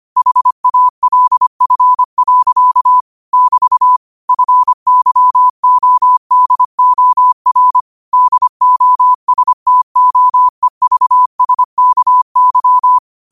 Morse code, heard as a keyed tone, about 25 wpm, 1000 Hz. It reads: HALF. <BT> FYODOR DOSTOEVSKY
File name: qotdfiles / 25wpm